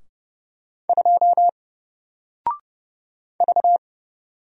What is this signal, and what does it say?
Morse code, heard as a keyed tone, about 30 wpm, 700 Hz. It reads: 2 V
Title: Single Character (Letter:Number:Punctuation:Prosign) 30wpm